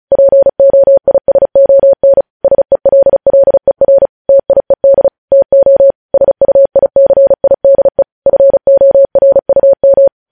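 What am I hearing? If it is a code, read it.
Morse code, heard as a keyed tone, about 36 wpm, 550 Hz. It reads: POISON SELLER TIED TO SUICIDE FORUM
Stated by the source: Morse code practice sending news headlines at 35 wpm.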